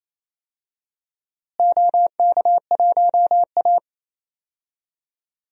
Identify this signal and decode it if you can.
Morse code, heard as a keyed tone, about 27 wpm, 700 Hz. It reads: OK1A
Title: DXCC-100 Call Signs - 4 Characters - Repeated 2x - 28wpm